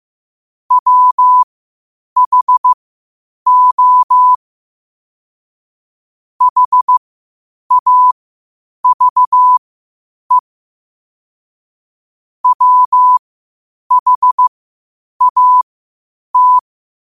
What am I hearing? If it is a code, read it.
Morse code, heard as a keyed tone, about 15 wpm, spaced as for 8 wpm, 1000 Hz. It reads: WHO HAVE WHAT
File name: qotdfiles / 8wpm